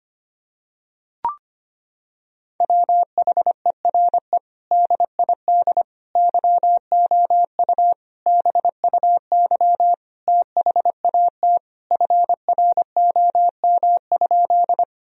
Morse code, heard as a keyed tone, about 25 wpm, 700 Hz. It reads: WHERE DID YOU BUY THAT FROM?
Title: Sentences from Top 1000 Words - Review Part 4 - 25wpm